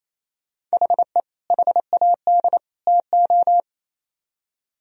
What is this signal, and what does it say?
Morse code, heard as a keyed tone, about 36 wpm, 700 Hz. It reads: HE HAD TO